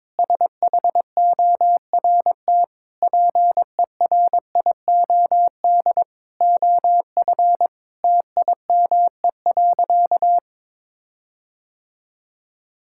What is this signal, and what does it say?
Morse code, heard as a keyed tone, about 22 wpm, 700 Hz. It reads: SHORT PERIOD OF TIME.
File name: Sentences with 7-Word Phrases - 22wpm